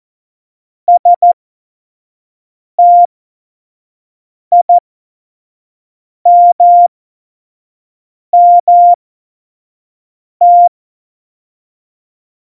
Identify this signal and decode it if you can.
Morse code, heard as a keyed tone, about 14 wpm, spaced as for 5 wpm, 700 Hz. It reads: STIMMT
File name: Morsekurs / Witze / Witz 22. Oktober 2019, 25 F70